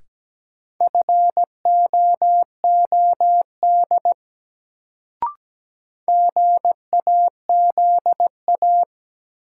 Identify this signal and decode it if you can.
Morse code, heard as a keyed tone, about 17 wpm, 700 Hz. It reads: FOOD GAZA
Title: Sets of 5 Words - New Word Drill - 17wpm